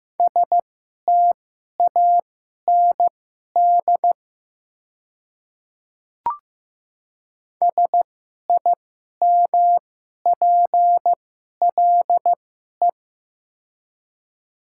Morse code, heard as a keyed tone, about 15 wpm, spaced as for 11 wpm, 700 Hz. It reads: STAND SIMPLE
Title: 2x Character Spacing - Top 500 Words - 15wpm